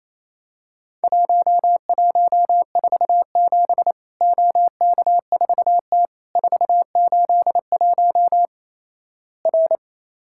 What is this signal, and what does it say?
Morse code, heard as a keyed tone, about 28 wpm, 700 Hz. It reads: 1147 OK4T 481 R
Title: Worked All Europe DX Contest - QTC Exchange - 28wpm